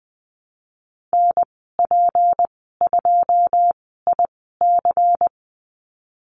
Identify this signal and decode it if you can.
Morse code, heard as a keyed tone, about 20 wpm, spaced as for 15 wpm, 700 Hz. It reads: NP2IC
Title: 179 Lesson 29g - Slash - Call Signs 20wpm